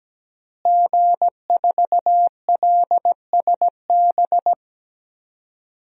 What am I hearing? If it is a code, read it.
Morse code, heard as a keyed tone, about 17 wpm, 700 Hz. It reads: G4LSB
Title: DXCC-100 Call Signs - 5 Characters - Repeated 3x - 17wpm